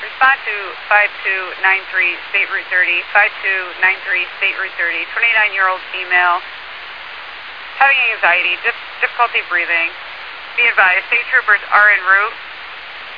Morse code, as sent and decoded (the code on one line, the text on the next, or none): none